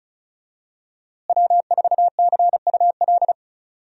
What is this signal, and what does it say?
Morse code, heard as a keyed tone, about 34 wpm, 700 Hz. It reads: W4CUL
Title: USA Call Signs - 5 Characters - Repeated 2x - 35wpm